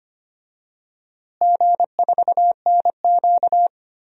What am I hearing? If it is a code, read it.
Morse code, heard as a keyed tone, about 25 wpm, 700 Hz. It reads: G4NQ
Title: DXCC-100 Call Signs - 4 Characters - Repeated 2x - 25wpm